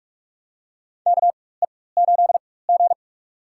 Morse code, heard as a keyed tone, about 45 wpm, spaced as for 21 wpm, 700 Hz. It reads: KE8G
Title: USA Call Signs - 4 Characters - 4x Character Spacing - 45wpm